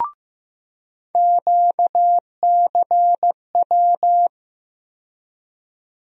Morse code, heard as a keyed tone, about 15 wpm, 700 Hz. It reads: QCW
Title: USA Call Sign Postfixes - 3 Letters 15wpm